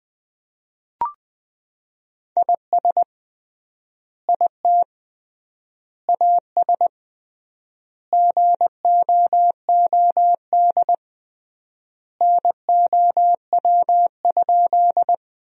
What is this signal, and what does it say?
Morse code, heard as a keyed tone, about 20 wpm, 700 Hz. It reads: IS IT AS GOOD NOW?
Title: 3x Word-Spacing - Sentences from Top 200 Words - 20wpm